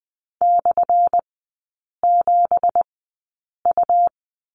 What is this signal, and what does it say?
Morse code, heard as a keyed tone, about 20 wpm, spaced as for 8 wpm, 700 Hz. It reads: /7U
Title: 178 Lesson 29e - Slash - Three Characters 20wpm